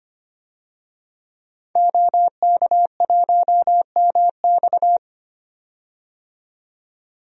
Morse code, heard as a keyed tone, about 25 wpm, 700 Hz. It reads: OK1MX
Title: DXCC-100 Call Signs - 5 Characters - 25wpm